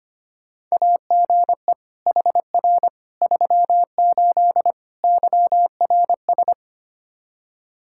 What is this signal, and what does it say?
Morse code, heard as a keyed tone, about 25 wpm, 700 Hz. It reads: AGE HR 38 YRS
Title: K1USN SST QSO Practice 25wpm